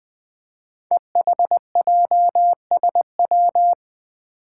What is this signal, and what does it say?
Morse code, heard as a keyed tone, about 20 wpm, 700 Hz. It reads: EHJSW
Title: Code Groups - 5 Letters - 20wpm